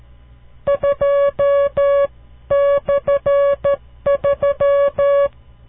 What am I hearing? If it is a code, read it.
Morse code, heard as a keyed tone, about 12 wpm, 560 Hz. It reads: 2/3
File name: CW Learning Programs / farnsworth / unit 5